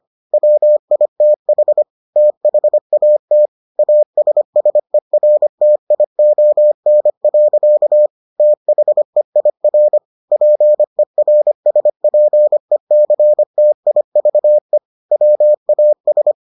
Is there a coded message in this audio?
Yes: Morse code from 0.3 to 16.3 s, heard as a keyed tone, about 25 wpm, 600 Hz. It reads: WITH THAT ASSERTION. THEIR PERSPECTIVE WAS